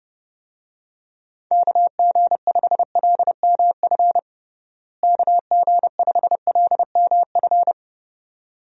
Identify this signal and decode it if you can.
Morse code, heard as a keyed tone, about 30 wpm, 700 Hz. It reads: KG5LMF KG5LMF
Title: USA Call Signs - 6 Characters - Repeated 2x - 30wpm